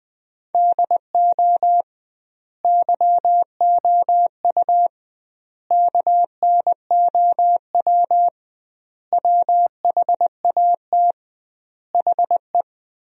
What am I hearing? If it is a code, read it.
Morse code, heard as a keyed tone, about 21 wpm, 700 Hz. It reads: DO YOU KNOW WHAT HE
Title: 2x Word-Spacing - Sentences from Top 300 Words - 20wpm